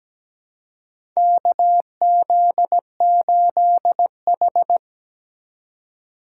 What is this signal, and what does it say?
Morse code, heard as a keyed tone, about 17 wpm, 700 Hz. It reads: KZ8H
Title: USA Call Signs - 4x Character Spacing - 17wpm